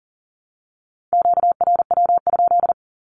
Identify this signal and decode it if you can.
Morse code, heard as a keyed tone, about 40 wpm, 700 Hz. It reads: QRW?
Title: Q-Codes Repeated 3x 40wpm